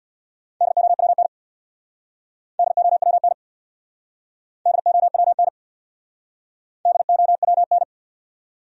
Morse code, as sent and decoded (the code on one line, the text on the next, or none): -.. --- .-- -. / -.. --- .-- -. / -.. --- .-- -. / -.. --- .-- -.
DOWN DOWN DOWN DOWN